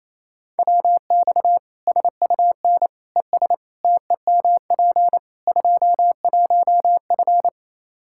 Morse code, heard as a keyed tone, about 28 wpm, 700 Hz. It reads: WX SUN ES TEMP 21F
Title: Practice QSO - Exchange 2 - Info - 28wpm